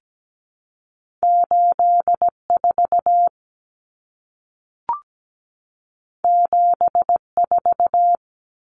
Morse code, heard as a keyed tone, about 17 wpm, 700 Hz. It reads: 84 74
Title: Common Cut Numbers - 2-Digit 17wpm